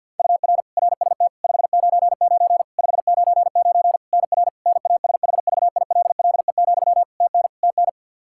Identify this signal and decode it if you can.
Morse code, heard as a keyed tone, about 51 wpm, 700 Hz. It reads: UR RST 599 599 NR NASHVILLE, TN TN
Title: Practice QSO - Exchange 1 - Beginning - 50wpm